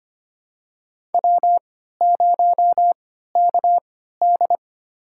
Morse code, heard as a keyed tone, about 25 wpm, spaced as for 14 wpm, 700 Hz. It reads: W0KD
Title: USA Call Signs - 4 Characters - 3x Character Spacing - 25wpm